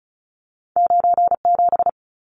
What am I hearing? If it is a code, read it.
Morse code, heard as a keyed tone, about 36 wpm, 700 Hz. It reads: 97
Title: Numbers - 2-Digit 35wpm